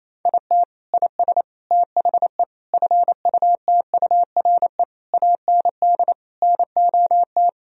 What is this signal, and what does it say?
Morse code, heard as a keyed tone, about 28 wpm, 700 Hz. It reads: IT IS THE FUTURE AND NOT
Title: Sentences from Top 1000 Words - Review Part 4 - 28wpm